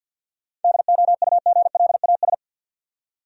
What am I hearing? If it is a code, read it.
Morse code, heard as a keyed tone, about 49 wpm, 700 Hz. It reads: DOUGLAS
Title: USA Common Names - 50wpm